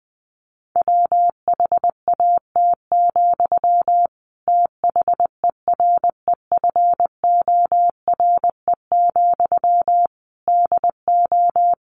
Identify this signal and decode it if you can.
Morse code, heard as a keyed tone, about 20 wpm, 700 Hz. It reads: WHAT, THEREFORE, DO
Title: Sentences from Top 1100 Words 20wpm